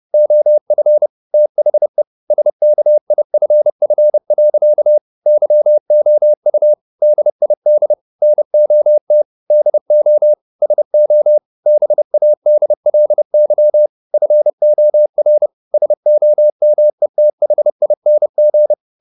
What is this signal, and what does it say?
Morse code, heard as a keyed tone, about 30 wpm, 600 Hz. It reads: OF THE SKIFF. YOU DID NOT DO SO BADLY FOR SOMETHING